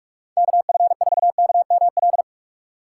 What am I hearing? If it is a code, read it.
Morse code, heard as a keyed tone, about 60 wpm, spaced as for 53 wpm, 700 Hz. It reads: KF4KML
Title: Speed Racing - USA 2x3 Call Signs 45wpm